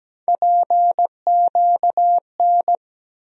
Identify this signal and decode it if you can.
Morse code, heard as a keyed tone, about 17 wpm, 700 Hz. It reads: PQN
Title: USA Call Sign Postfixes - 3 Letters 17wpm